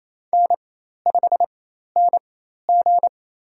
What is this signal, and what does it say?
Morse code, heard as a keyed tone, about 28 wpm, spaced as for 13 wpm, 700 Hz. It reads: N5NG
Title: USA Call Signs - 4 Characters - 4x Character Spacing - 28wpm